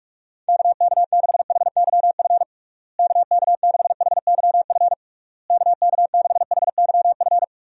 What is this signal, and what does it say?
Morse code, heard as a keyed tone, about 48 wpm, 700 Hz. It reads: KK6HYF KK6HYF KK6HYF
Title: USA Call Signs - 6 Characters - Repeated 3x - 45wpm